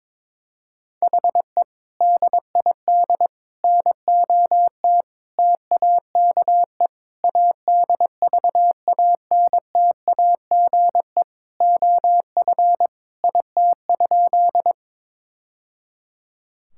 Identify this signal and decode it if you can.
Morse code, heard as a keyed tone, about 22 wpm, 700 Hz. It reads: HE DID NOT TAKE ADVANTAGE OF IT?
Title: Sentences from Top 1000 Words - Review Part 6 - 22wpm